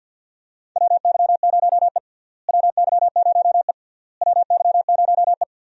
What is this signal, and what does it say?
Morse code, heard as a keyed tone, about 52 wpm, 700 Hz. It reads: WY0E WY0E WY0E
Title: USA Call Signs - 4 Characters - Repeated 3x - 50wpm